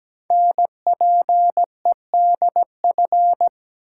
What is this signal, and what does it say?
Morse code, heard as a keyed tone, about 17 wpm, 700 Hz. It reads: NPEDF
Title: Code Groups - 5 Letter:Numbers - 17wpm